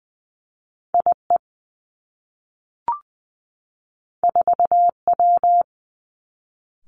Morse code, heard as a keyed tone, about 20 wpm, 700 Hz. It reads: I E 4 W
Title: Two Challenging Letter-Number Combinations 20wpm